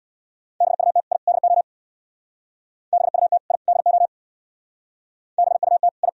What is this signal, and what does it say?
Morse code, heard as a keyed tone, about 63 wpm, 700 Hz. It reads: BUT I DO BUT I DO BUT I